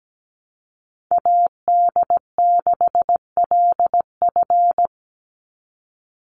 Morse code, heard as a keyed tone, about 17 wpm, 700 Hz. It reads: AD6LF
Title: USA Call Signs 17wpm